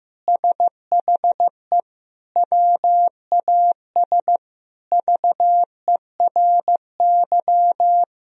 Morse code, heard as a keyed tone, about 15 wpm, 700 Hz. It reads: SHE WAS VERY
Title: Sentences from Top 1000 Words - Review Part 4 - 15wpm